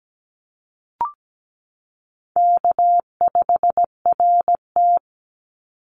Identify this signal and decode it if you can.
Morse code, heard as a keyed tone, about 17 wpm, 700 Hz. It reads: K5RT
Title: Easy Call Signs Repeated 3x - 17wpm